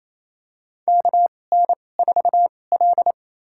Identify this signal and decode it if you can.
Morse code, heard as a keyed tone, about 28 wpm, spaced as for 20 wpm, 700 Hz. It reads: KN4L
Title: USA Call Signs - 4 Characters - 2x Character Spacing - 28wpm